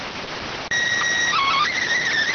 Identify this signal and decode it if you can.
SSTV (slow-scan television), PD 290.